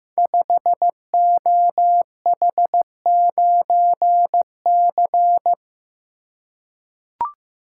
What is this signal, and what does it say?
Morse code, heard as a keyed tone, about 15 wpm, 700 Hz. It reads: 5OH9C
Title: Code Groups - 5 Letter:Numbers - 15wpm